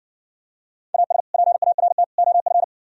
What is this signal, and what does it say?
Morse code, heard as a keyed tone, about 59 wpm, 700 Hz. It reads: AS PART OF